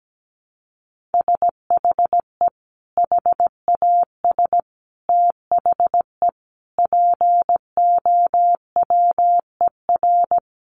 Morse code, heard as a keyed tone, about 17 wpm, 700 Hz. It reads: SHE HAS THE POWER
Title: Sentences from Top 1200 Words 17wpm